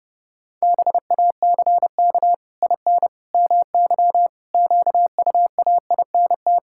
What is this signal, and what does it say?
Morse code, heard as a keyed tone, about 30 wpm, 700 Hz. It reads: BACK IN MY QUAINT
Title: Pangram Challenge v2 - 30wpm